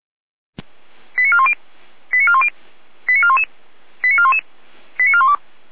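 Five-tone selective call (ZVEI), heard as quick runs of five tones, tones of about 70 ms. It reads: 89410 89410 89411 89411 89412
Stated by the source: ZVEI five-tone call 89412, received by SDR.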